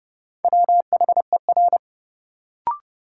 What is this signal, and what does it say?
Morse code, heard as a keyed tone, about 30 wpm, 700 Hz. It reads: WHER
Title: Sentences from Top 1500 Words - New Word Drill - 30wpm